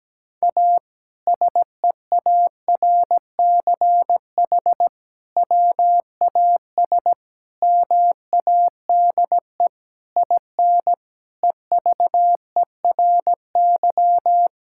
Morse code, heard as a keyed tone, about 17 wpm, 700 Hz. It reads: A SEARCH WAS MADE IN EVERY
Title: Sentences from Top 1000 Words - Review Part 5 - 17wpm